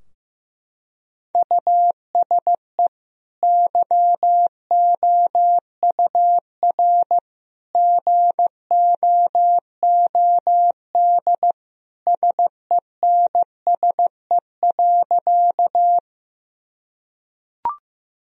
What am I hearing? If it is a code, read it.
Morse code, heard as a keyed tone, about 15 wpm, 700 Hz. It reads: USE YOUR GOOD SENSE.
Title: Sentences from Top 1000 Words - Review Part 4 - 15wpm